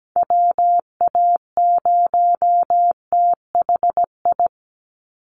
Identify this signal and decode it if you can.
Morse code, heard as a keyed tone, about 17 wpm, 700 Hz. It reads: WA0THI
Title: USA Call Signs 17wpm